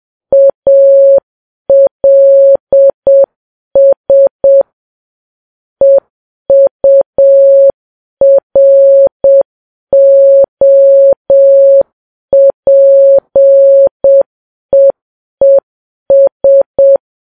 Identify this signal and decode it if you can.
Morse code, heard as a keyed tone, about 7 wpm, 550 Hz. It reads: ALS EUROPEES
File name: Files Web tls / Morse on9fv / morse7kps